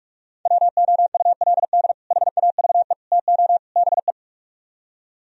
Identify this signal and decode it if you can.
Morse code, heard as a keyed tone, about 59 wpm, 700 Hz. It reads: WOULD HAVE TO BE